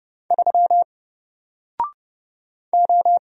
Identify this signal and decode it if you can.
Morse code, heard as a keyed tone, about 30 wpm, 700 Hz. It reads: 3 O
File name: Single-Letter-Number-30wpm